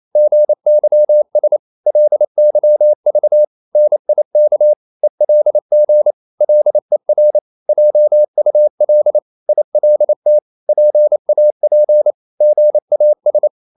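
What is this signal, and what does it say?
Morse code, heard as a keyed tone, about 28 wpm, 600 Hz. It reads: GYS LYV NIK ELG LER JUL ILT PAP GAS
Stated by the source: Danish short words sent in Morse at 28 wpm, from CW med Gnister.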